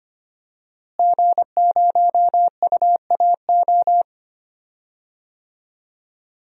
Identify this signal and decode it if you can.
Morse code, heard as a keyed tone, about 25 wpm, 700 Hz. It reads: G0UAO
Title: DXCC-100 Call Signs - 5 Characters - Repeated 2x - 25wpm